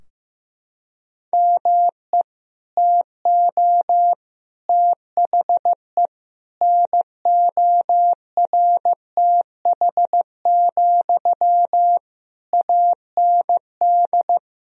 Morse code, heard as a keyed tone, about 16 wpm, 700 Hz. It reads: ME TO THE NORTH, AND
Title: Sentences from Top 1000 Words - Review Part 2 - 15wpm